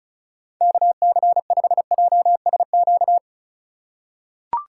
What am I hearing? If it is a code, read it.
Morse code, heard as a keyed tone, about 34 wpm, 690 Hz. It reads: KC5JSQ
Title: USA Call Signs - 3x Character Spacing - 35wpm